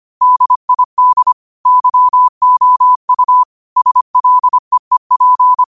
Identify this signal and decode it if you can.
Morse code, heard as a keyed tone, about 25 wpm, 1000 Hz. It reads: DID YOU SLEEP
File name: qotdfiles / 25wpm